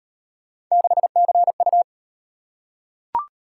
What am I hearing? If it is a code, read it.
Morse code, heard as a keyed tone, about 38 wpm, 700 Hz. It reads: 6CU